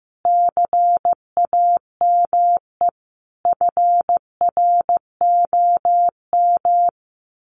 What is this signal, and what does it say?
Morse code, heard as a keyed tone, about 14 wpm, 700 Hz. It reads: CAME FROM
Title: Sentences from Top 1500 Words 15wpm